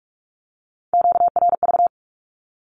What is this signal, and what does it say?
Morse code, heard as a keyed tone, about 46 wpm, 700 Hz. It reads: QRV
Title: Q-Codes with Spoken Meaning - 45wpm